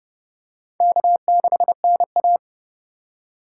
Morse code, heard as a keyed tone, about 30 wpm, 700 Hz. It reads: K6NA
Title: K1USN SST QSO Practice - 30wpm